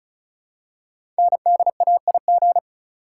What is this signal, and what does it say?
Morse code, heard as a keyed tone, about 36 wpm, 700 Hz. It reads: NDAIG
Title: Code Groups - 5 Letters - 35wpm